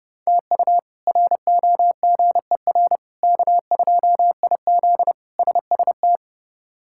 Morse code, heard as a keyed tone, about 32 wpm, 700 Hz. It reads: TU ROGER K2IZ SST
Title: K1USN SST QSO Practice - 30wpm